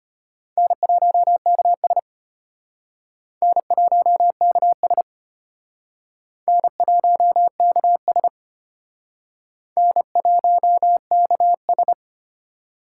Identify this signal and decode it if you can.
Morse code, heard as a keyed tone, about 31 wpm, 700 Hz. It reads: N1KS N1KS N1KS N1KS
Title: Speed Racing - USA 1x2 Call Signs 25wpm